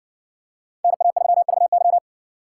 Morse code, heard as a keyed tone, about 60 wpm, 700 Hz. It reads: NA3VY